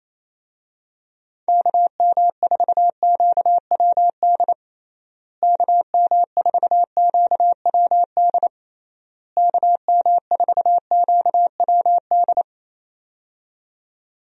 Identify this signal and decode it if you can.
Morse code, heard as a keyed tone, about 28 wpm, 700 Hz. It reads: KM4QWD KM4QWD KM4QWD
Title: USA Call Signs - 6 Characters - Repeated 3x - 28wpm